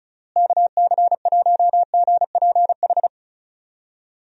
Morse code, heard as a keyed tone, about 34 wpm, 700 Hz. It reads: KC1GPH
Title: USA Call Signs - 6 Characters - Repeated 3x - 35wpm